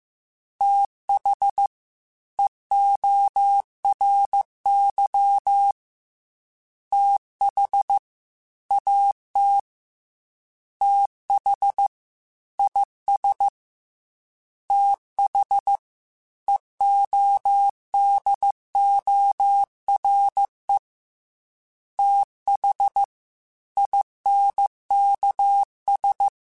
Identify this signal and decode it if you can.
Morse code, heard as a keyed tone, about 15 wpm, 800 Hz. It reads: TH EORY TH AT TH IS TH EODORE TH INKS
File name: TH Runs sp / TH Runs sp.mp3s / TH Runs sp @13